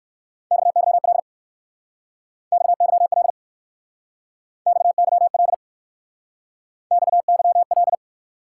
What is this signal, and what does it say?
Morse code, heard as a keyed tone, about 56 wpm, 700 Hz. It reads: XYL XYL XYL XYL
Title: Speed-Racing Top 90 QSO Elements 45wpm